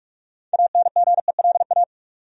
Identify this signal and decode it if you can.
Morse code, heard as a keyed tone, about 46 wpm, 700 Hz. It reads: ANGELA
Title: USA Common Names - 45wpm